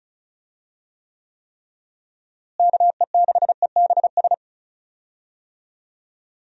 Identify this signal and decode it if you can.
Morse code, heard as a keyed tone, about 36 wpm, 700 Hz. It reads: KE6EBS
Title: USA Call Signs - 6 Characters - Repeated 3x - 35wpm